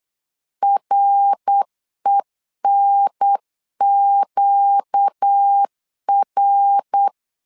Morse code, heard as a keyed tone, about 8 wpm, 800 Hz. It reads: RENQR
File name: LEZIONI / 4 Q1 8wpm 3035